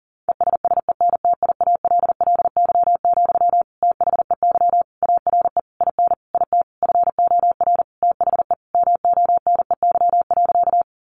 Morse code, heard as a keyed tone, about 40 wpm, 700 Hz. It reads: ESSENTIALLY, THEY ARE IN IT FOR THE MONEY.
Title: Sentences from Top 1300 Words 40wpm